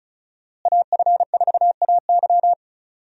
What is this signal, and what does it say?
Morse code, heard as a keyed tone, about 34 wpm, 700 Hz. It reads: AF4AY
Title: USA Call Signs - 5 Characters - Repeated 3x - 35wpm